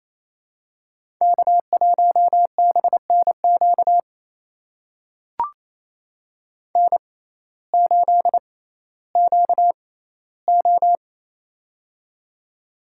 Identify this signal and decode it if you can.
Morse code, heard as a keyed tone, about 28 wpm, spaced as for 15 wpm, 700 Hz. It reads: K1BNQ N8QO
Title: USA Call Signs - 6x Character Spacing - 28wpm